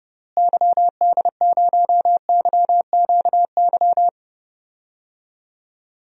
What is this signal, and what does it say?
Morse code, heard as a keyed tone, about 31 wpm, 700 Hz. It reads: YD0YQY
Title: DXCC-100 Call Signs - 6 Characters - Repeated 2x - 30wpm